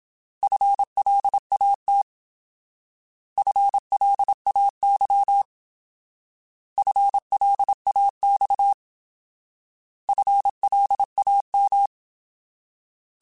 Morse code, heard as a keyed tone, about 27 wpm, 800 Hz. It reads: FLAT FLAY FLAX FLAM
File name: Alphabetical / Alphabetical mp3s / Fs @25